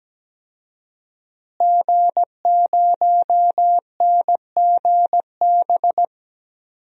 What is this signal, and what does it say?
Morse code, heard as a keyed tone, about 17 wpm, 700 Hz. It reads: G0NGB
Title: DXCC-100 Call Signs - 5 Characters - Repeated 3x - 17wpm